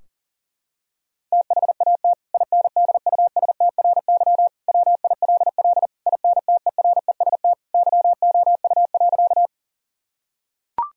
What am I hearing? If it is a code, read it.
Morse code, heard as a keyed tone, about 37 wpm, 700 Hz. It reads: THAT INDUSTRY WILL INTEREST YOU.
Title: Sentences from Top 1000 Words - Review Part 4 - 40wpm